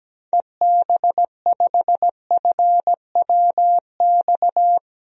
Morse code, heard as a keyed tone, about 17 wpm, 700 Hz. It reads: EB5FWX
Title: DXCC-100 Call Signs - 6 Characters - 17wpm